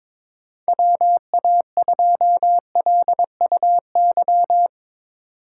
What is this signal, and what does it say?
Morse code, heard as a keyed tone, about 21 wpm, 700 Hz. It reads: WA2LUY
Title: USA Call Signs - 6 Characters - 8x Character Spacing - 22wpm